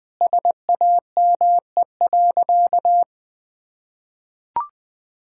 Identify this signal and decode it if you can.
Morse code, heard as a keyed tone, about 20 wpm, 700 Hz. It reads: SAME.
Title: Sentences with 2-Word Phrases - 20wpm